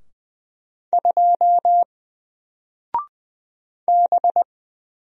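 Morse code, heard as a keyed tone, about 20 wpm, 700 Hz. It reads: 2 B
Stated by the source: Single letter and number Morse drill at 20 wpm, from N4LCD.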